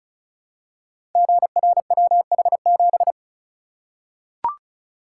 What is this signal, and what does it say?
Morse code, heard as a keyed tone, about 36 wpm, 700 Hz. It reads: GRWH7
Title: Code Groups - 5 Letter:Numbers - 35wpm